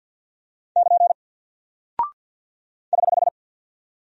Morse code, heard as a keyed tone, about 48 wpm, 700 Hz. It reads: <KN> <HH>
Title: Single Character (Letter:Number:Punctuation:Prosign) 50wpm